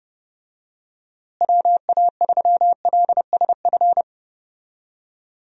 Morse code, heard as a keyed tone, about 31 wpm, 700 Hz. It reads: WA3LSF
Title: USA Call Signs - 6 Characters - Repeated 2x - 30wpm